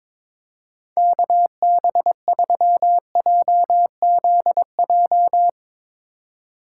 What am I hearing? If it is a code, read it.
Morse code, heard as a keyed tone, about 22 wpm, 700 Hz. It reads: KB3JZJ
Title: Speed Racing - USA 2x3 Call Signs 22wpm